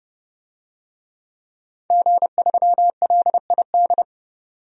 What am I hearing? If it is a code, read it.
Morse code, heard as a keyed tone, about 30 wpm, 700 Hz. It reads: G3LID
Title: DXCC-100 Call Signs - 5 Characters - 30wpm